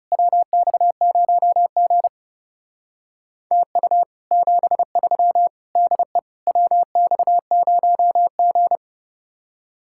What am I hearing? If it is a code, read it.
Morse code, heard as a keyed tone, about 34 wpm, 700 Hz. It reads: WX0G TU 73 DE WX0G
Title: Speed-Racing Common QSO Phrases - 30wpm